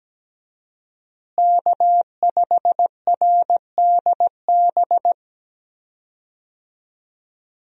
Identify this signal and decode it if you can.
Morse code, heard as a keyed tone, about 17 wpm, 700 Hz. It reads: K5RDB
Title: USA Call Signs - 5 Characters - Repeated 3x - 17wpm